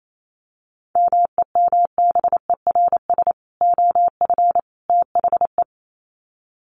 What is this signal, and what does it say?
Morse code, heard as a keyed tone, about 28 wpm, 700 Hz. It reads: MEMBERS OF THE
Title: Sets of 3 Words 28wpm